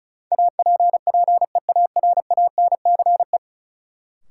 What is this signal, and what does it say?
Morse code, heard as a keyed tone, about 34 wpm, 700 Hz. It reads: APPEARANCE
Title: Sentences from Top 1100 Words - New Word Drill - 35wpm